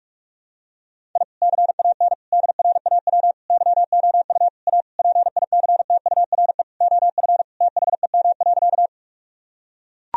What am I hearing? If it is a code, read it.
Morse code, heard as a keyed tone, about 41 wpm, 700 Hz. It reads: I CAN DRAW YOU A PICTURE OF THEM.
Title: Sentences from Top 1000 Words - Review Part 1 - 45wpm